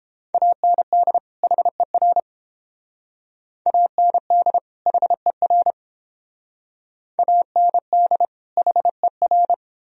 Morse code, heard as a keyed tone, about 29 wpm, 700 Hz. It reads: AND HER AND HER AND HER